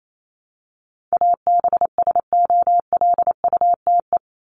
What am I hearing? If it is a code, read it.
Morse code, heard as a keyed tone, about 28 wpm, 700 Hz. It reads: ABSOLUTE
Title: LY Suffix - Top 100 Words with Base Words 28wpm